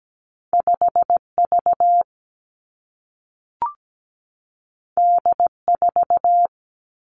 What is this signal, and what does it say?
Morse code, heard as a keyed tone, about 17 wpm, 700 Hz. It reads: 5 V D 4
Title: Two Challenging Letter-Number Combinations 17wpm